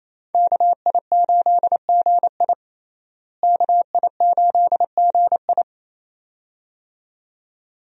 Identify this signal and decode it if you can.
Morse code, heard as a keyed tone, about 28 wpm, 700 Hz. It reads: KI8GI KI8GI
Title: USA Call Signs - 5 Characters - Repeated 2x - 28wpm